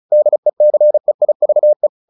Morse code, heard as a keyed tone, about 35 wpm, 600 Hz. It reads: DECEIVE
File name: the old man and the sea 2 35wpm